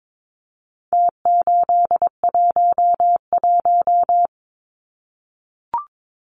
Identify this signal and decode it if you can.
Morse code, heard as a keyed tone, about 22 wpm, 700 Hz. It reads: T811
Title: Numbers - Contest Style - 4-Digit 22wpm